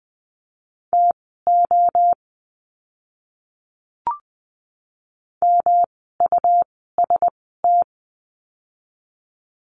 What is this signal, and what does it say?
Morse code, heard as a keyed tone, about 20 wpm, spaced as for 14 wpm, 700 Hz. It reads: TO MUST
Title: 237 Lesson 38f - Letter q - Words 20wpm